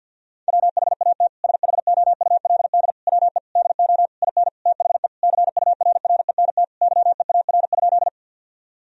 Morse code, heard as a keyed tone, about 50 wpm, 700 Hz. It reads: WHAT SHOULD WE DO IN THE CURRENT YEAR?
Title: Sentences from Top 1000 Words - Review Part 4 - 50wpm